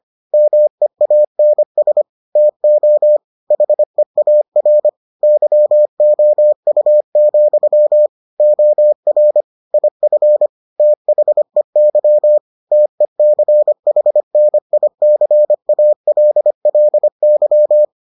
Morse code, heard as a keyed tone, about 25 wpm, 600 Hz. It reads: MEANS TO HEAR YOU, OR IF THEY TECHNICALLY